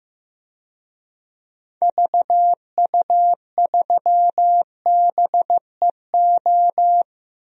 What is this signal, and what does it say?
Morse code, heard as a keyed tone, about 15 wpm, 700 Hz. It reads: VU3BEO
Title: DXCC-100 Call Signs - 6 Characters - Repeated 2x - 15wpm